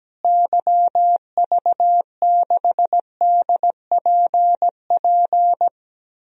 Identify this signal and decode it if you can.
Morse code, heard as a keyed tone, about 17 wpm, 700 Hz. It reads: YV6DPP
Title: DXCC-100 Call Signs - 6 Characters - Repeated 2x - 17wpm